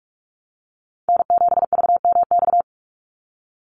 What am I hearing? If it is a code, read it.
Morse code, heard as a keyed tone, about 44 wpm, 700 Hz. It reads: N7VMX
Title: USA Call Signs 45wpm